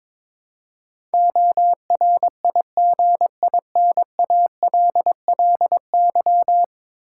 Morse code, heard as a keyed tone, about 22 wpm, 700 Hz. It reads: ORIGINALLY
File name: practice / 22wpm / 10 Letter Words - First 3 Letters Given - 22wpm